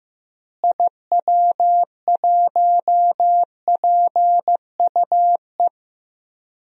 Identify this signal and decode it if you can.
Morse code, heard as a keyed tone, about 15 wpm, 700 Hz. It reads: IW1PUE
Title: DXCC-100 Call Signs - 6 Characters - Repeated 3x - 15wpm